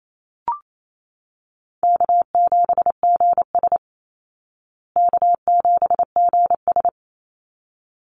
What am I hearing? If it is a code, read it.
Morse code, heard as a keyed tone, about 28 wpm, 700 Hz. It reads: K7GS K7GS
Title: Call Signs Repeated 3x - 28wpm